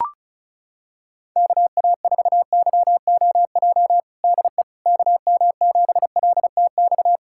Morse code, heard as a keyed tone, about 36 wpm, 700 Hz. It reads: KA4YOJ DE KM7LTX
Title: USA Callsign DE Callsign 35wpm